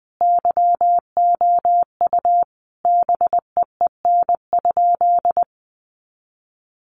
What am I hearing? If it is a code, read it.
Morse code, heard as a keyed tone, about 20 wpm, 700 Hz. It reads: YOU BEEN?
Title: Sentences from Top 400 Words 20wpm